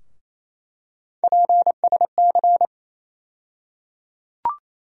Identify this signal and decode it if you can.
Morse code, heard as a keyed tone, about 28 wpm, 700 Hz. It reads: PSC
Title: USA Call Sign Postfixes - 3 Letters 28wpm